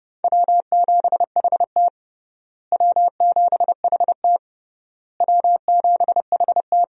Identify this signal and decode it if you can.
Morse code, heard as a keyed tone, about 30 wpm, 700 Hz. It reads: W7HT W7HT W7HT
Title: USA Call Signs - 4 Characters - Repeated 3x - 30wpm